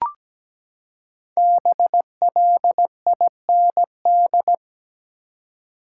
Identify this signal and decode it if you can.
Morse code, heard as a keyed tone, about 17 wpm, 700 Hz. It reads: BLIND
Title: Five Letter Words - 17wpm